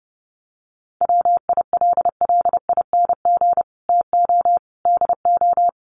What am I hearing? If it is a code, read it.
Morse code, heard as a keyed tone, about 30 wpm, 700 Hz. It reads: WILLING TO DO
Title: Sentences from Top 1400 Words 30wpm